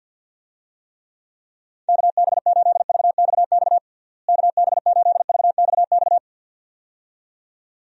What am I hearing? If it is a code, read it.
Morse code, heard as a keyed tone, about 49 wpm, 700 Hz. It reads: KB8VXX KB8VXX
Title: USA Call Signs - 6 Characters - Repeated 2x - 50wpm